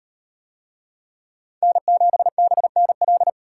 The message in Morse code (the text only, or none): N7BNL